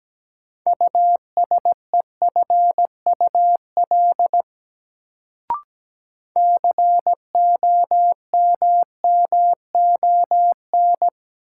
Morse code, heard as a keyed tone, about 17 wpm, 700 Hz. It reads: USEFUL COMMON
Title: Sentences from Top 700 Words - New Word Drill - 17wpm